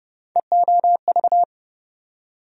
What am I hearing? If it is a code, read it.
Morse code, heard as a keyed tone, about 30 wpm, 700 Hz. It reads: EOV
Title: USA Call Sign Postfixes - 3 Letters 30wpm